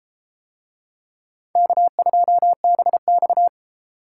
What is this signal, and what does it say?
Morse code, heard as a keyed tone, about 30 wpm, 700 Hz. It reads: K2BX